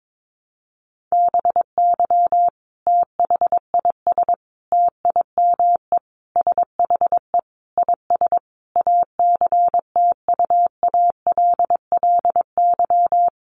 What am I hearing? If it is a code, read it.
Morse code, heard as a keyed tone, about 22 wpm, 700 Hz. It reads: BY THIS TIME SHE IS ACTUALLY
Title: Sentences from Top 700 Words 22wpm